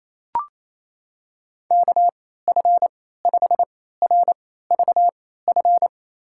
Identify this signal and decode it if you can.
Morse code, heard as a keyed tone, about 28 wpm, spaced as for 16 wpm, 700 Hz. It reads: KF5RVF
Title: USA Call Signs - 6 Characters - 3x Character Spacing - 28wpm